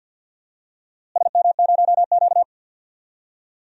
Morse code, heard as a keyed tone, about 52 wpm, 700 Hz. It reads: SM0Q
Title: DXCC-100 Call Signs - 4 Characters - Repeated 2x - 50wpm